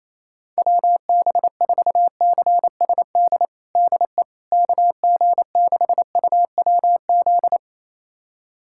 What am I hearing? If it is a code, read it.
Morse code, heard as a keyed tone, about 27 wpm, 700 Hz. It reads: WB4CSD DE KG6UWZ
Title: USA Callsign DE Callsign 28wpm